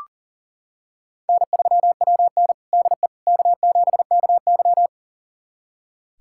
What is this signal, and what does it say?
Morse code, heard as a keyed tone, about 40 wpm, 700 Hz. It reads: N3WN DE K7KY
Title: USA Callsign DE Callsign - 4 Characters - 40wpm